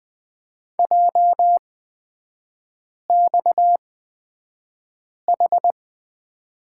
Morse code, heard as a keyed tone, about 20 wpm, spaced as for 5 wpm, 690 Hz. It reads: JXH
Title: 229 Lesson 37e - Letter x - Three Characters 20wpm